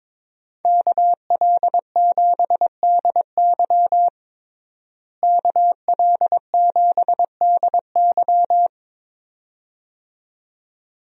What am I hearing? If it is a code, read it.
Morse code, heard as a keyed tone, about 22 wpm, 700 Hz. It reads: KL7DY KL7DY
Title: DXCC-100 Call Signs - 5 Characters - Repeated 2x - 22wpm